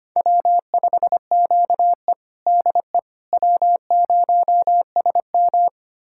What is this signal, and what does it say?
Morse code, heard as a keyed tone, about 25 wpm, 700 Hz. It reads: W5QE DE W0SM
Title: USA Callsign DE Callsign - 4 Characters - 25wpm